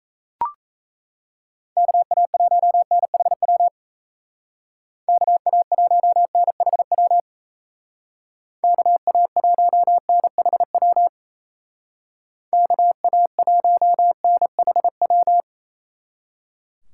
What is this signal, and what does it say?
Morse code, heard as a keyed tone, about 35 wpm, 700 Hz. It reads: KA1NHW KA1NHW KA1NHW KA1NHW
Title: Speed Racing - USA 2x3 Call Signs 28wpm